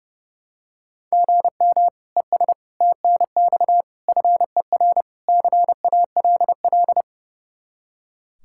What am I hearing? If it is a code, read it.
Morse code, heard as a keyed tone, about 30 wpm, 700 Hz. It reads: GM ES TNX FER CALL
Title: Practice QSO - Exchange 1 - Beginning - 30wpm